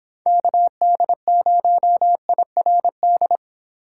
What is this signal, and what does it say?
Morse code, heard as a keyed tone, about 26 wpm, 700 Hz. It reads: KD0IRD